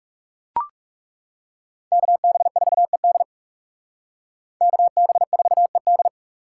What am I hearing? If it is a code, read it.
Morse code, heard as a keyed tone, about 38 wpm, 700 Hz. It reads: KB4ED KB4ED